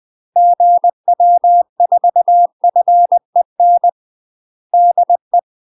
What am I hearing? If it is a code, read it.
Morse code, heard as a keyed tone, about 20 wpm, 700 Hz. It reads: GW4FEN DE
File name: UK MASTER QSOs TRAINING / 20WPM / UK MASTER QSOs 0026